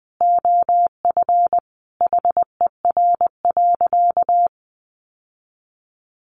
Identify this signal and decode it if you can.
Morse code, heard as a keyed tone, about 20 wpm, 700 Hz. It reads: OF HER.
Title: Sentences from Top 1100 Words 20wpm